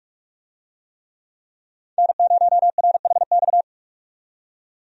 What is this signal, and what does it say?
Morse code, heard as a keyed tone, about 46 wpm, 700 Hz. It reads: N0RHX
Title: USA Call Signs - 5 Characters - 45wpm